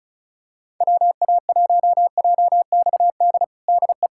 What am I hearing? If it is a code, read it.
Morse code, heard as a keyed tone, about 34 wpm, 690 Hz. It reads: WA1JXD DE
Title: USA Callsign DE Callsign - 6 Characters - 35wpm